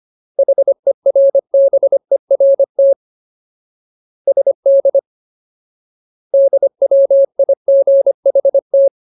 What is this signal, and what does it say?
Morse code, heard as a keyed tone, about 25 wpm, 550 Hz. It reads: HERBERT SD DWIGHT
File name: Ultimate Word List / 25WPM Ultimate